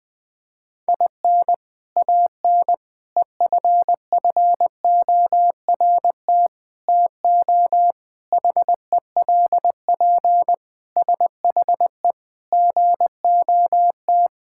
Morse code, heard as a keyed tone, about 20 wpm, 700 Hz. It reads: IN AN EFFORT TO HELP SHE GOT
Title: Sentences with 4-Word Phrases - 20wpm